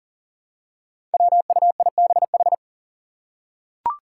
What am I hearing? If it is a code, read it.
Morse code, heard as a keyed tone, about 40 wpm, 690 Hz. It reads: WUIBH
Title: Five Challenging Letter-Number Combinations 40wpm